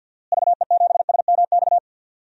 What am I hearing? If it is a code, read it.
Morse code, heard as a keyed tone, about 47 wpm, 700 Hz. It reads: VE7SMX
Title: DXCC-100 Call Signs - 6 Characters - Repeated 2x - 50wpm